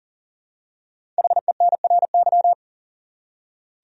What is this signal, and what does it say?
Morse code, heard as a keyed tone, about 39 wpm, 700 Hz. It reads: HENRY
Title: USA Common Names - 40wpm